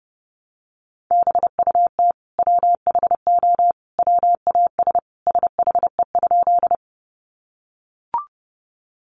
Morse code, heard as a keyed tone, about 30 wpm, 700 Hz. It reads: BUT WHO WAS SHE?
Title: Sentences from Top 100 Words 30wpm